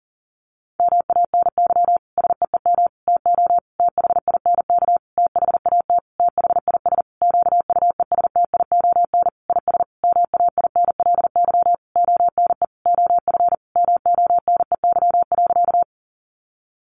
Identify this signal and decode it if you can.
Morse code, heard as a keyed tone, about 40 wpm, 700 Hz. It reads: MANY SEEM TO THINK THAT THIS QUESTION IS MAINLY ONE OF MONEY.
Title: Sentences from Top 1400 Words 40wpm